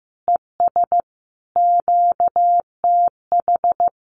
Morse code, heard as a keyed tone, about 15 wpm, 700 Hz. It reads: ES QTH
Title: Common QSO Phrases Encore 15wpm